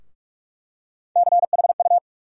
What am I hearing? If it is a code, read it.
Morse code, heard as a keyed tone, about 45 wpm, 700 Hz. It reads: CHU
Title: USA Call Sign Postfixes - 3 Letters 45wpm